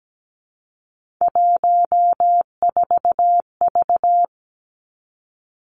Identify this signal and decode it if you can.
Morse code, heard as a keyed tone, about 17 wpm, 700 Hz. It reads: 14V
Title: Three Challenging Letter-Number Combinations 17wpm